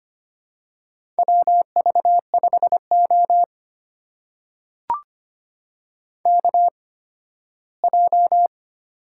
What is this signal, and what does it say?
Morse code, heard as a keyed tone, about 25 wpm, spaced as for 15 wpm, 700 Hz. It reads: WV5O KJ
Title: USA Call Signs - 8x Character Spacing - 25wpm